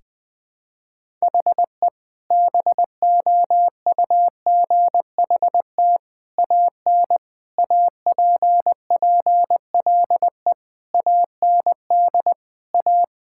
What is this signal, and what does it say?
Morse code, heard as a keyed tone, about 20 wpm, 700 Hz. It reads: HE BOUGHT AN APPLE AND A
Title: Sentences with 2-Word Phrases - 20wpm